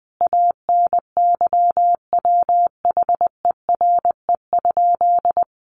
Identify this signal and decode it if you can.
Morse code, heard as a keyed tone, about 20 wpm, 700 Hz. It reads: ANYWHERE?
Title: Sentences from Top 1400 Words 20wpm